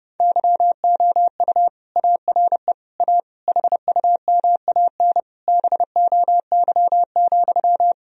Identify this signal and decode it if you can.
Morse code, heard as a keyed tone, about 32 wpm, 700 Hz. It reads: YOU ARE A HUMAN BOY,
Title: Sentences from Top 1000 Words - Review Part 3 - 30wpm